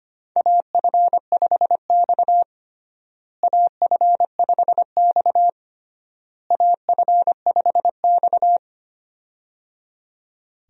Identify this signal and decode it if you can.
Morse code, heard as a keyed tone, about 25 wpm, 700 Hz. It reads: AF5X AF5X AF5X
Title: USA Call Signs - 4 Characters - Repeated 3x - 25wpm